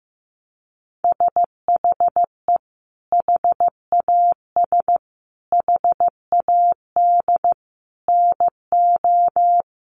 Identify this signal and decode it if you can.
Morse code, heard as a keyed tone, about 15 wpm, 700 Hz. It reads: SHE HAS HAD NO
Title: Sentences from Top 1400 Words 15wpm